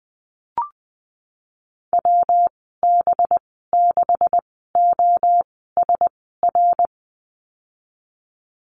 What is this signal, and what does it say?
Morse code, heard as a keyed tone, about 20 wpm, spaced as for 14 wpm, 700 Hz. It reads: WB6OSR
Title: 186 Lesson 30g - Letter b - Call Signs 20wpm